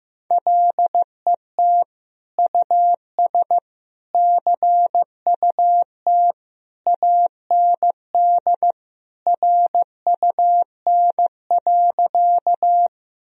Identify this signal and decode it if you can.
Morse code, heard as a keyed tone, about 15 wpm, 700 Hz. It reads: LET US CUT AND RUN.
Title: Sentences from Top 1000 Words - Review Part 1 - 15wpm